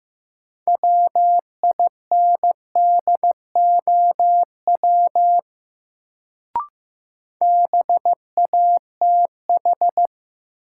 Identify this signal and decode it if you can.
Morse code, heard as a keyed tone, about 15 wpm, 700 Hz. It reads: WINDOW BATH
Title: Sentences from Top 1600 Words - New Word Drill Alt - 15wpm